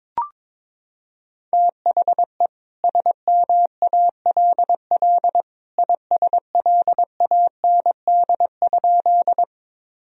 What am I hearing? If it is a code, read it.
Morse code, heard as a keyed tone, about 22 wpm, 700 Hz. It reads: THE SMALL ISLAND?
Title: Sentences from Top 1000 Words - Review Part 3 - 22wpm